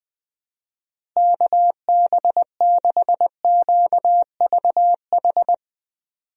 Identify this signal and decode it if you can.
Morse code, heard as a keyed tone, about 20 wpm, 700 Hz. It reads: KB6QVH
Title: USA Call Signs - 6 Characters - Repeated 3x - 20wpm